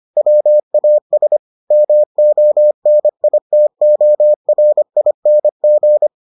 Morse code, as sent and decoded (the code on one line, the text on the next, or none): .-- .- ... / -- --- -. .. - --- .-. .. -. --.
WAS MONITORING